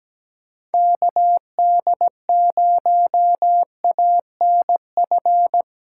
Morse code, heard as a keyed tone, about 17 wpm, 700 Hz. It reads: KD0ANF
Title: USA Callsign DE Callsign - 6 Characters - 17wpm